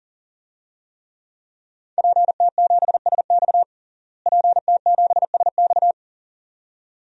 Morse code, heard as a keyed tone, about 40 wpm, 700 Hz. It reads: PT7SX PT7SX
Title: DXCC-100 Call Signs - 5 Characters - Repeated 2x - 40wpm